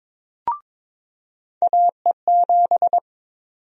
Morse code, heard as a keyed tone, about 23 wpm, 700 Hz. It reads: AE7
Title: USA Call Sign Prefixes - 22wpm